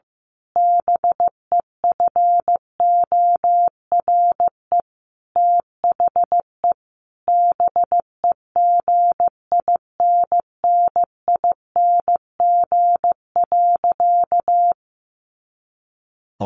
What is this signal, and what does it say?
Morse code, heard as a keyed tone, about 15 wpm, 700 Hz. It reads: BEFORE THE BEGINNING.
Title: A Sound Of Thunder 15wpm